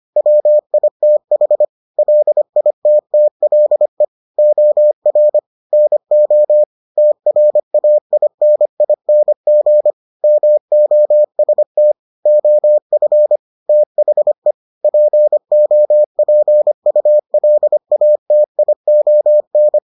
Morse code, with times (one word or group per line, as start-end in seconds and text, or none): 0.2-1.6 WITH
2.0-4.0 LITTLE
4.4-5.4 OR
5.7-6.6 NO
7.0-9.9 TRAINING
10.2-11.9 MOST
12.3-13.4 OF
13.7-14.5 THE
14.8-19.8 POPULATION